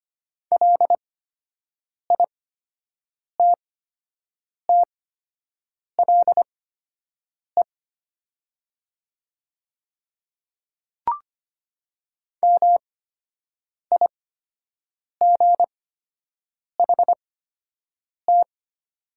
Morse code, heard as a keyed tone, about 25 wpm, spaced as for 7 wpm, 700 Hz. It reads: LITTLE MIGHT
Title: 8x Character Spacing - Top 300 Words - 25wpm